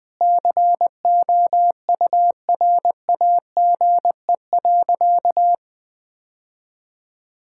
Morse code, heard as a keyed tone, about 20 wpm, 700 Hz. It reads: COURAGE.
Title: Sentences with 4-Word Phrases - Spoken Beginning - 20wpm